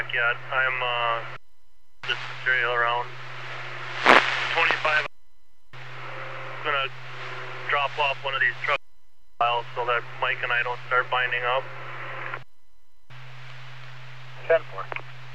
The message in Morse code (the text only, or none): none